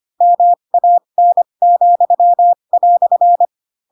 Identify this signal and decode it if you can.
Morse code, heard as a keyed tone, about 25 wpm, spaced as for 22 wpm, 700 Hz. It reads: MAN,"
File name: Pinocchio / Pinocchio ch3 18 25 ik1hge